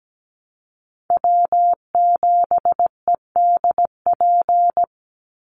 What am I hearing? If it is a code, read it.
Morse code, heard as a keyed tone, about 17 wpm, 700 Hz. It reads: W7EDP
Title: USA Call Signs 17wpm